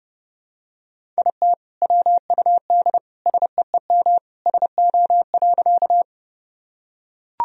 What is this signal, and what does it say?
Morse code, heard as a keyed tone, about 30 wpm, 700 Hz. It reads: IT WUD SEEM SO.
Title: Sentences from Top 300 Words and Abbreviations 30wpm